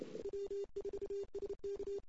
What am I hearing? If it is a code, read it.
Morse code, heard as a keyed tone, about 28 wpm, 400 Hz. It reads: W4SK
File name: CW / 2008-ss-cw-k5tr-op-n5rz -17:00